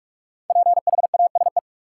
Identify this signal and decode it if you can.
Morse code, heard as a keyed tone, about 45 wpm, 700 Hz. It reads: PHASE
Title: Sentences from Top 1200 Words - New Word Drill - 45wpm